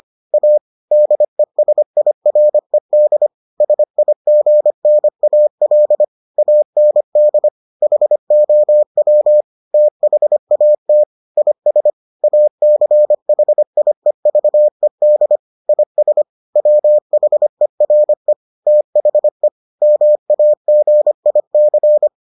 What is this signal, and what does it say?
Morse code, heard as a keyed tone, about 25 wpm, 600 Hz. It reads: A DESIRED SIGNAL AND HOW THAT IS ACHIEVED IS WHERE THE MAGIC